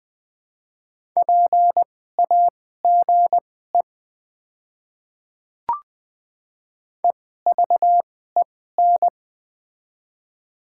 Morse code, heard as a keyed tone, about 20 wpm, spaced as for 14 wpm, 700 Hz. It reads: PAGE EVEN
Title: 2x Character Spacing - Top 300 Words - 20wpm